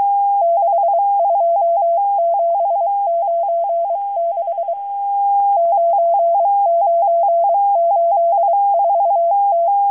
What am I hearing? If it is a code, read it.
Morse code, heard as a keyed tone, about 23 wpm, 680 Hz. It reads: 62796 9984T
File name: HM02- -0437utc-7351khz